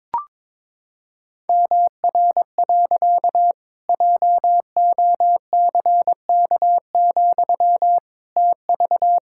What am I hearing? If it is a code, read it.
Morse code, heard as a keyed tone, about 22 wpm, 700 Hz. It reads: MR. JOCK, TV
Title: Pangram Challenge v2 - 22wpm